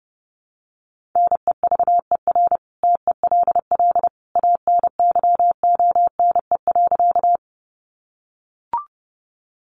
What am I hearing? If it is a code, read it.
Morse code, heard as a keyed tone, about 30 wpm, 700 Hz. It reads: NEVER TELL ANYONE.
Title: Sentences from Top 900 Words 30wpm